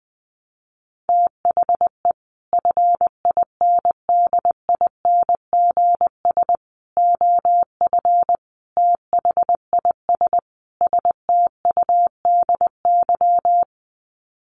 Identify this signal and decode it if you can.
Morse code, heard as a keyed tone, about 19 wpm, 700 Hz. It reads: THE FINDINGS OF THIS STUDY
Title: Sets of 5 Words Encore2 20wpm